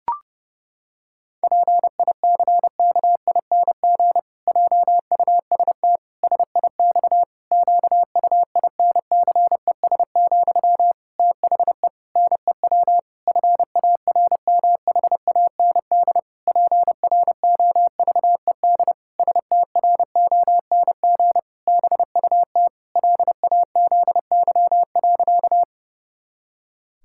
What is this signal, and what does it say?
Morse code, heard as a keyed tone, about 28 wpm, 700 Hz. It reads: PICKING JUST SIX QUINCES, THE NEW FARMHAND PROVED STRONG BUT LAZY.
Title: Pangram Challenge v2 - 30wpm